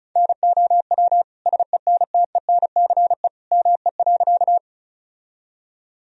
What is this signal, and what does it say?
Morse code, heard as a keyed tone, about 36 wpm, 700 Hz. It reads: NOW SENTENCE ME.
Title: Sentences from Top 1000 Words - Review Part 1 - 35wpm